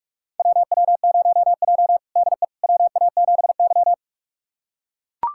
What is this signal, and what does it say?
Morse code, heard as a keyed tone, about 45 wpm, 700 Hz. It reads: WW0J DE WA7Y
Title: USA Callsign DE Callsign - 4 Characters - 45wpm